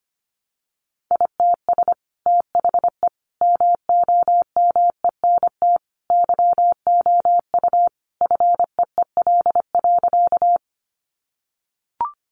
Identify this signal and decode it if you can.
Morse code, heard as a keyed tone, about 24 wpm, 700 Hz. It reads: ITS THE MOMENT YOU FEEL.
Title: Sentences from Top 1000 Words 25wpm